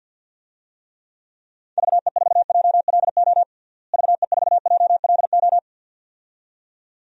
Morse code, heard as a keyed tone, about 50 wpm, 700 Hz. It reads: VE4JLO VE4JLO
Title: DXCC-100 Call Signs - 6 Characters - Repeated 2x - 50wpm